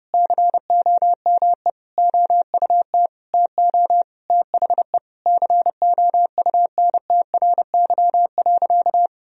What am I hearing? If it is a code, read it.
Morse code, heard as a keyed tone, about 32 wpm, 700 Hz. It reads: COME OUT TO THE COUNTRY.
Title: Sentences from Top 1000 Words - Review Part 1 - 30wpm